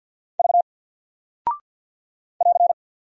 Morse code, heard as a keyed tone, about 48 wpm, 700 Hz. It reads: V <AR>
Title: Single Character (Letter:Number:Punctuation:Prosign) 50wpm